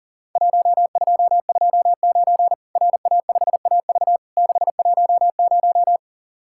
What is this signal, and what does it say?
Morse code, heard as a keyed tone, about 42 wpm, 700 Hz. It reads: 1229 RA5AV 610
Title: Worked All Europe DX Contest - QTC Exchange - 40wpm